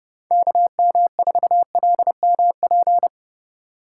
Morse code, heard as a keyed tone, about 30 wpm, 700 Hz. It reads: KM4LMP
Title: USA Call Signs - 6 Characters - 4x Character Spacing - 30wpm